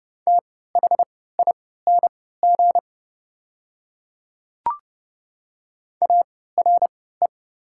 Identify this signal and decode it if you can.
Morse code, heard as a keyed tone, about 30 wpm, spaced as for 17 wpm, 700 Hz. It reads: THING ARE
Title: 3x Character Spacing - Top 200 Words - 30wpm